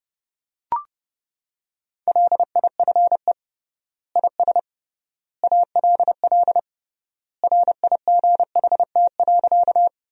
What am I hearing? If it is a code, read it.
Morse code, heard as a keyed tone, about 30 wpm, 700 Hz. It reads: LIFE IS ALL RIGHT.
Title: 3x Word-Spacing - Sentences from Top 300 Words - 30wpm